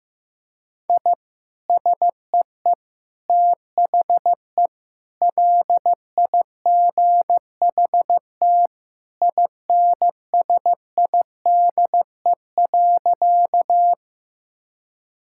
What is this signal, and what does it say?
Morse code, heard as a keyed tone, about 15 wpm, 700 Hz. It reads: I SEE THE LIGHT INSIDE.
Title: Sentences from Top 1000 Words - Review Part 6 - 15wpm